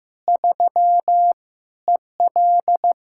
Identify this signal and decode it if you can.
Morse code, heard as a keyed tone, about 15 wpm, 700 Hz. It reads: 3 EL
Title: K1USN SST QSO Practice 15wpm